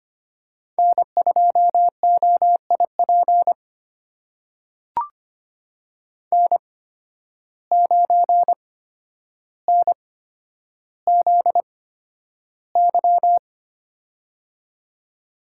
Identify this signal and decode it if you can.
Morse code, heard as a keyed tone, about 25 wpm, spaced as for 11 wpm, 700 Hz. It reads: N2OIP N9NZY
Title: USA Call Signs - 5 Characters - 8x Character Spacing - 25wpm